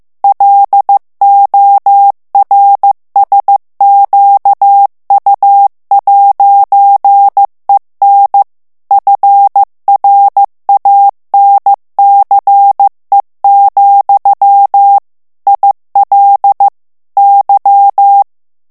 Morse code, heard as a keyed tone, about 15 wpm, 800 Hz. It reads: LORSQU'EN FRANCE, IL Y
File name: morse mp3 / FRANCAIS TEXTE CLAIR 1 VITESSE 14 MOTS MINUTES